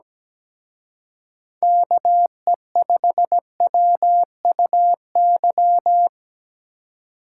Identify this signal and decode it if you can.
Morse code, heard as a keyed tone, about 20 wpm, 700 Hz. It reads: KE5WUY
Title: Speed Racing - USA 2x3 Call Signs 17wpm